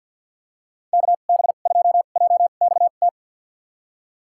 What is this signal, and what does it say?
Morse code, heard as a keyed tone, about 51 wpm, spaced as for 36 wpm, 700 Hz. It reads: KB2JXT
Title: USA Call Signs - 2x Character Spacing - 50wpm